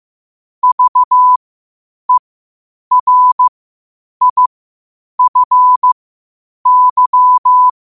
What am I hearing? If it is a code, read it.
Morse code, heard as a keyed tone, about 15 wpm, spaced as for 8 wpm, 1000 Hz. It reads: VERIFY
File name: qotdfiles / 8wpm